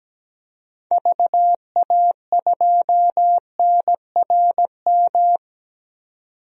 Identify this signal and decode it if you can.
Morse code, heard as a keyed tone, about 17 wpm, 700 Hz. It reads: VA2NRM
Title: DXCC-100 Call Signs - 6 Characters - 17wpm